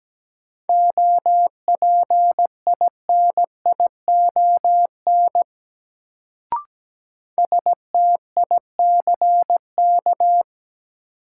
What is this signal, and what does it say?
Morse code, heard as a keyed tone, about 17 wpm, 700 Hz. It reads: OPINION STICK
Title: Sentences from Top 1200 Words - New Word Drill - 17wpm